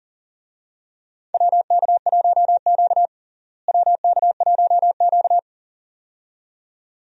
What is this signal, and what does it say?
Morse code, heard as a keyed tone, about 38 wpm, 700 Hz. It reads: WK1Q WK1Q
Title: USA Call Signs - 4 Characters - Repeated 2x - 40wpm